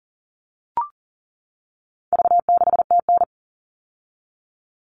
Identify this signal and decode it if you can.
Morse code, heard as a keyed tone, about 40 wpm, 700 Hz. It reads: V6TN
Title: All Cut Numbers - 4-Digit 40wpm